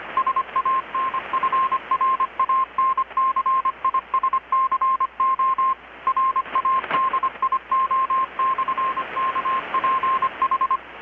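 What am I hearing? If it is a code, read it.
Morse code, heard as a keyed tone, about 25 wpm, 1060 Hz, with radio noise. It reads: SANFRANCISCO RADIO/KPH
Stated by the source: Morse from KPH's Night of Nights XX maritime radiotelegraphy event, received in Chiapas, Mexico about 3,800 km away on a Kenwood R-600 with a grounded 10 m long-wire antenna.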